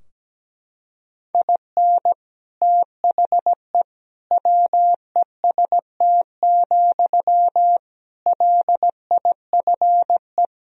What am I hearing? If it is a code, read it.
Morse code, heard as a keyed tone, about 17 wpm, 700 Hz. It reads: IN THE WEST, LIFE
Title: Sentences from Top 1000 Words - Review Part 2 - 17wpm